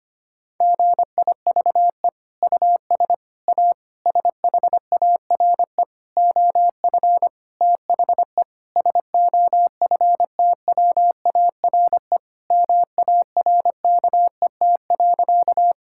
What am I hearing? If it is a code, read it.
Morse code, heard as a keyed tone, about 25 wpm, 700 Hz. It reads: GIVE US A SHARE OF THE SOFTWARE MARKET.
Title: Sentences from Top 1000 Words - Review Part 6 - 25wpm